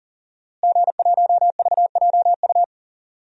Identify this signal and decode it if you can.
Morse code, heard as a keyed tone, about 40 wpm, 700 Hz. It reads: G1VJU
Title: DXCC-100 Call Signs - 5 Characters - Repeated 3x - 40wpm